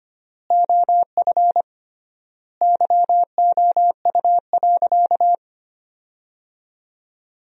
Morse code, heard as a keyed tone, about 25 wpm, 700 Hz. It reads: OF YOU.
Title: 3x Word-Spacing - Sentences from Top 200 Words - 25wpm